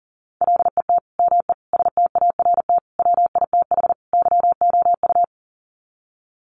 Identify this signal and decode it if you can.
Morse code, heard as a keyed tone, about 40 wpm, 700 Hz. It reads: LET ME START WITH YOU
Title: Sets of 5 Words Encore 40wpm